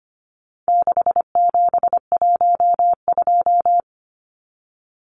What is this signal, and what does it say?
Morse code, heard as a keyed tone, about 25 wpm, 700 Hz. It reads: 6712
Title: Numbers - 4-Digit 25wpm